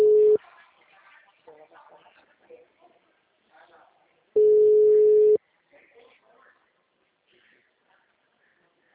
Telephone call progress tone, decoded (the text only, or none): ringback tone